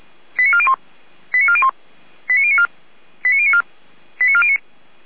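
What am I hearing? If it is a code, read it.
Five-tone selective call (ZVEI), heard as quick runs of five tones, tones of about 70 ms. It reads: 89401 89401 89094 89094 89409
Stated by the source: Five-tone ZVEI selective call received on an SDR, address 89356, channel 455.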